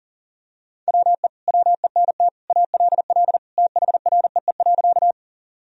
Morse code, heard as a keyed tone, about 40 wpm, 700 Hz. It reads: WE WENT ALL THREE.
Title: Sentences from Top 1000 Words - Review Part 1 - 40wpm